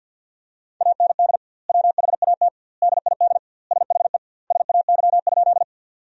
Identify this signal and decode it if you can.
Morse code, heard as a keyed tone, about 50 wpm, 700 Hz. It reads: AND WHAT DID SHE SAY?
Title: 2x Word-Spacing - Sentences from Top 200 Words - 50wpm